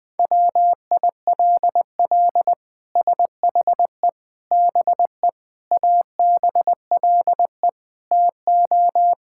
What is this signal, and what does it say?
Morse code, heard as a keyed tone, about 21 wpm, 700 Hz. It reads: WILL SHE BE ABLE TO
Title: Sentences from Top 1000 Words - Review Part 6 - 20wpm